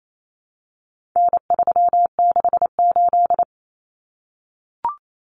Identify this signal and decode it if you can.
Morse code, heard as a keyed tone, about 28 wpm, 700 Hz. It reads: N368
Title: Common Cut Numbers - 4-Digit 28wpm